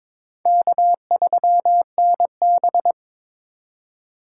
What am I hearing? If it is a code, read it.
Morse code, heard as a keyed tone, about 22 wpm, 700 Hz. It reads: K3NB
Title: USA Call Signs - 4 Characters - 3x Character Spacing - 22wpm